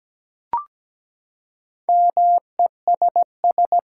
Morse code, heard as a keyed tone, about 17 wpm, 700 Hz. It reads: MESS
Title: Four Letter Words - 17wpm